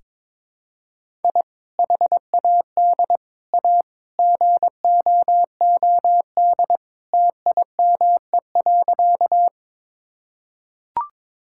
Morse code, heard as a keyed tone, about 22 wpm, 700 Hz. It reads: I HAD A GOOD TIME.
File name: practice / 22wpm / Sentences with 3-Word Phrases - 22wpm